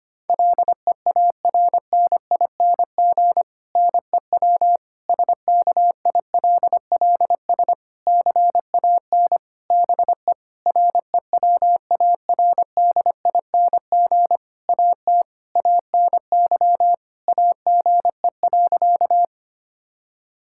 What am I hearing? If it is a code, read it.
Morse code, heard as a keyed tone, about 25 wpm, 700 Hz. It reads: LEARNING NEW SKILLS CAN BE REWARDING AT ANY AGE.
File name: practice / 25wpm / Difficult Sentences with Hints - 25wpm